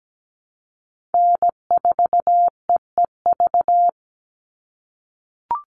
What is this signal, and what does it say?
Morse code, heard as a keyed tone, about 17 wpm, 700 Hz. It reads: N4EEV
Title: Call Signs - 17wpm